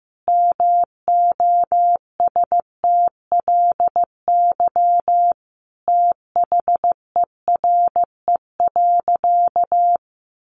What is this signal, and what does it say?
Morse code, heard as a keyed tone, about 14 wpm, 700 Hz. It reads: MOSTLY THERE.
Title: Sentences from Top 1300 Words 15wpm